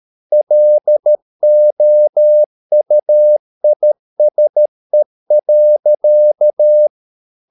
Morse code, heard as a keyed tone, about 13 wpm, 600 Hz. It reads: LOUISE.
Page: Ferie 13 wpm